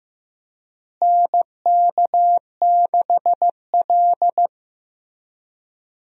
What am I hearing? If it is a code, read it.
Morse code, heard as a keyed tone, about 15 wpm, 700 Hz. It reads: NK6L
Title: USA Call Signs - 4 Characters - Repeated 2x - 15wpm